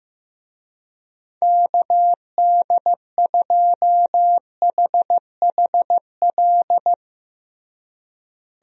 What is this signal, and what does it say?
Morse code, heard as a keyed tone, about 16 wpm, 700 Hz. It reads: KD2HHL
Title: USA Call Signs - 6 Characters - Repeated 3x - 15wpm